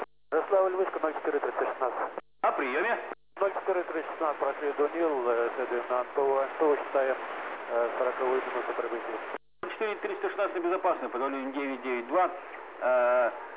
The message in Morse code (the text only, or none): none